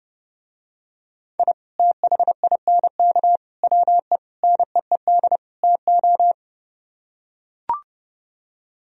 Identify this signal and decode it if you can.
Morse code, heard as a keyed tone, about 30 wpm, 700 Hz. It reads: I THINK WE NEED TO